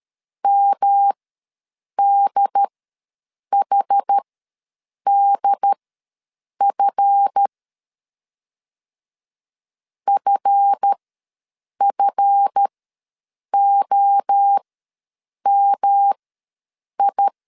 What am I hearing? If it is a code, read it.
Morse code, heard as a keyed tone, about 13 wpm, spaced as for 7 wpm, 800 Hz. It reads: MDHDF FFOMI
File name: LEZIONI / 7 GF9Z / 7 GF ws 12wpm 3140